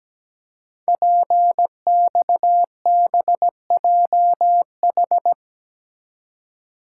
Morse code, heard as a keyed tone, about 17 wpm, 700 Hz. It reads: PXBJH
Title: Code Groups - 5 Letters - 17wpm